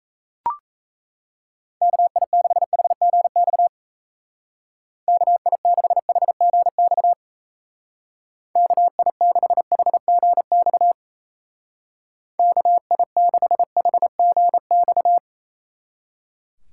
Morse code, heard as a keyed tone, about 33 wpm, 700 Hz. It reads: KI6HGX KI6HGX KI6HGX KI6HGX
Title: Speed Racing - USA Call Signs 28wpm